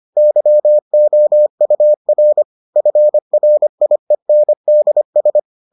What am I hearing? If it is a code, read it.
Morse code, heard as a keyed tone, about 25 wpm, 600 Hz. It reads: YOUR FRIENDS